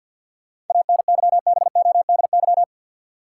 Morse code, heard as a keyed tone, about 49 wpm, 700 Hz. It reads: ANYBODY
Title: Sets of 5 Words - New Word Drill - 50wpm